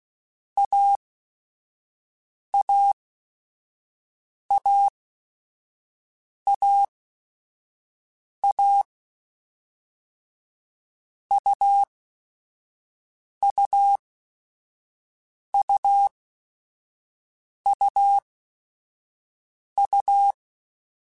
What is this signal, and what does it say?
Morse code, heard as a keyed tone, about 16 wpm, spaced as for 5 wpm, 800 Hz. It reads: AAAAA UUUUU
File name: Short Runs / Short Runs-mp3s / AUV AUV4 @5